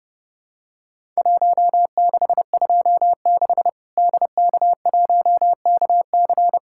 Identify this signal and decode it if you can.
Morse code, heard as a keyed tone, about 32 wpm, 700 Hz. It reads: 1626 DK1KC
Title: Worked All Europe DX Contest - QTC Exchange - 30wpm